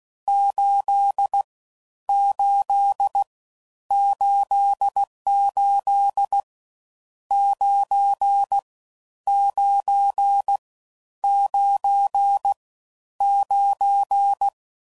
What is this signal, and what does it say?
Morse code, heard as a keyed tone, about 16 wpm, 800 Hz. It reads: 8 8 88 9 9 9 9
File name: Mixed / Misc / Numbs